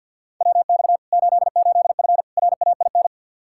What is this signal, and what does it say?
Morse code, heard as a keyed tone, about 50 wpm, 700 Hz. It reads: WX 88F RAIN
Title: Practice QSO - Exchange 2 - Info - 50wpm